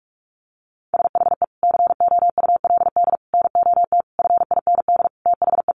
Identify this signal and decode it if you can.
Morse code, heard as a keyed tone, about 44 wpm, 700 Hz. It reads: SHE COULD NOT FIND THE
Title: Sentences from Top 1100 Words 45wpm